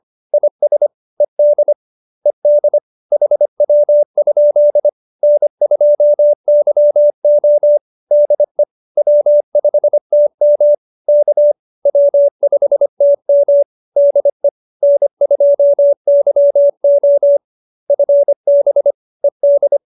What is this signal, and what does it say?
Morse code, heard as a keyed tone, about 25 wpm, 570 Hz. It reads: IS ED ED HW? N2YO DE W5TM K W5TM DE N2YO FB ED